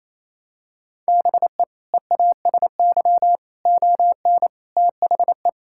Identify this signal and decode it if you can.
Morse code, heard as a keyed tone, about 28 wpm, 700 Hz. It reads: BE EASY ON THE
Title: Sentences from Top 1600 Words 28wpm